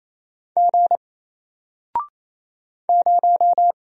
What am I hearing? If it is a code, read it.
Morse code, heard as a keyed tone, about 28 wpm, 700 Hz. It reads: G 0
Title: Single Character (Letter:Number:Punctuation:Prosign) Alt 28wpm